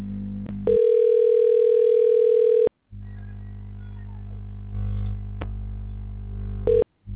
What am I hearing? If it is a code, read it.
Ringback tone (North America), 440 + 480 Hz.